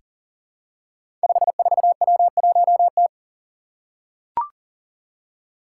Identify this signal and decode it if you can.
Morse code, heard as a keyed tone, about 40 wpm, 700 Hz. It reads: 54W1T
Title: Five Challenging Letter-Number Combinations 40wpm